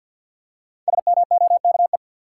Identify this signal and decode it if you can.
Morse code, heard as a keyed tone, about 52 wpm, 700 Hz. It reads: SMOKE
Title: Five Letter Words - 50wpm